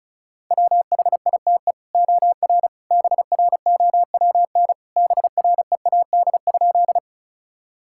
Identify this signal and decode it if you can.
Morse code, heard as a keyed tone, about 36 wpm, 700 Hz. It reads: WHITE OR BROWN BREAD?
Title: Sentences from Top 1600 Words 35wpm